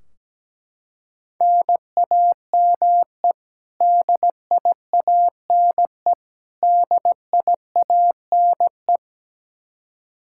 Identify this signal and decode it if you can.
Morse code, heard as a keyed tone, about 17 wpm, 700 Hz. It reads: NAME DIANE DIANE
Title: Practice QSO - Exchange 1 - Beginning - 17wpm